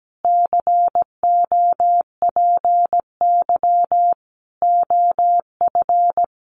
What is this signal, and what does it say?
Morse code, heard as a keyed tone, about 17 wpm, 700 Hz. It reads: COPY OF
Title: Sentences from Top 1400 Words 17wpm